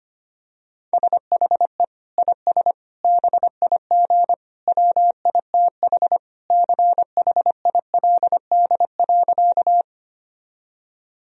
Morse code, heard as a keyed tone, about 25 wpm, 700 Hz. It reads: SHE IS BIG WITH CHILD.
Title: Sentences from Top 1000 Words - Review Part 1 - 25wpm